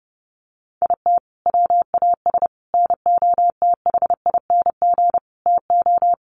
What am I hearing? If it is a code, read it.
Morse code, heard as a keyed tone, about 30 wpm, 700 Hz. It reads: IT WAS NOTHING TO
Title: Sentences from Top 1300 Words 30wpm